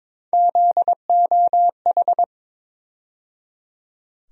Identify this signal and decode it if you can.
Morse code, heard as a keyed tone, about 22 wpm, 700 Hz. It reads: ZOH
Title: USA Call Sign Postfixes - 3 Letters 22wpm